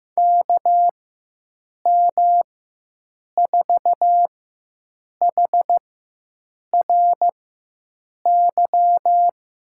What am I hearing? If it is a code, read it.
Morse code, heard as a keyed tone, about 16 wpm, spaced as for 7 wpm, 700 Hz. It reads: KM4HRY
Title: USA Call Signs - 4x Character Spacing - 15wpm